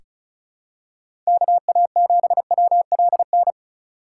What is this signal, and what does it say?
Morse code, heard as a keyed tone, about 36 wpm, 700 Hz. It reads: KA7WLN
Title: USA Call Signs - 6 Characters - 3x Character Spacing - 35wpm